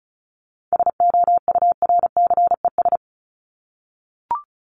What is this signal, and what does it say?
Morse code, heard as a keyed tone, about 34 wpm, 700 Hz. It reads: SOURCES
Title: ES:S Suffix - Top 100 Words with Base Words 35wpm